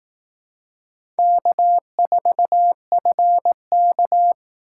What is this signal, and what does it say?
Morse code, heard as a keyed tone, about 18 wpm, 700 Hz. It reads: K4FK
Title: Speed Racing - USA 1x2 Call Signs 15wpm